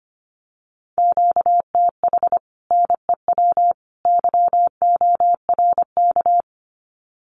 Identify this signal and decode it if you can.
Morse code, heard as a keyed tone, about 25 wpm, 700 Hz. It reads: QTH NEW YORK
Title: Common QSO Phrases 25wpm